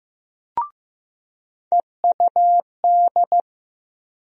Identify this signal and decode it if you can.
Morse code, heard as a keyed tone, about 15 wpm, 700 Hz. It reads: EUD
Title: USA Call Sign Postfixes - 3 Letters 15wpm